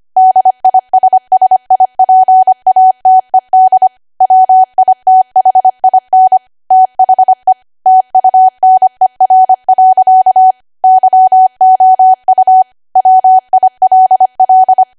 Morse code, heard as a keyed tone, about 25 wpm, 750 Hz. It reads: DISSIPATED WITHIN THE TUNER. YOU WILL
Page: 25 WPM Code Archive